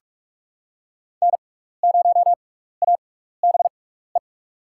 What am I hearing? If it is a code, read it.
Morse code, heard as a keyed tone, about 44 wpm, spaced as for 15 wpm, 700 Hz. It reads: N0ABE
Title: USA Call Signs - 5 Characters - 6x Character Spacing - 45wpm